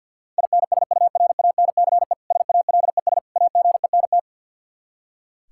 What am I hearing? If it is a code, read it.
Morse code, heard as a keyed tone, about 50 wpm, 700 Hz. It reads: INSURANCE SALES AGENT
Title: Practice QSO - Exchange 3 - More Info - 50wpm